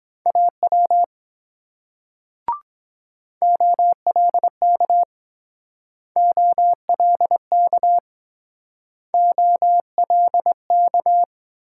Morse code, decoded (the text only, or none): AW OLK OLK OLK